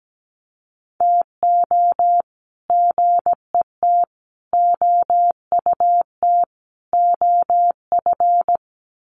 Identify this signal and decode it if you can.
Morse code, heard as a keyed tone, about 17 wpm, 700 Hz. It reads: TO GET OUT OF
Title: Sets of 4 Words 17wpm